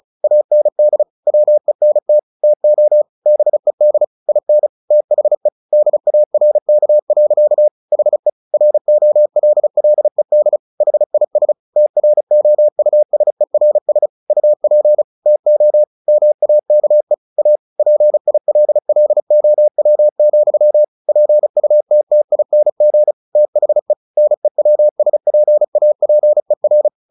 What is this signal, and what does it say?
Morse code, heard as a keyed tone, about 35 wpm, 600 Hz. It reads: AND WENT TO BED IN THE DARK. HE ROLLED HIS TROUSERS UP TO MAKE A PILLOW, PUTTING THE NEWSPAPER
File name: the old man and the sea / the old man and the sea 1 35wpm